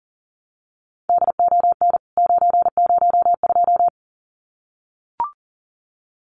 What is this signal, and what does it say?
Morse code, heard as a keyed tone, about 40 wpm, 700 Hz. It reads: DON 902
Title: CWT Exchange 40wpm